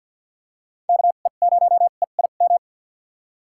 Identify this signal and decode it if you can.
Morse code, heard as a keyed tone, about 50 wpm, spaced as for 36 wpm, 700 Hz. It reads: KE0EIM
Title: USA Call Signs - 2x Character Spacing - 50wpm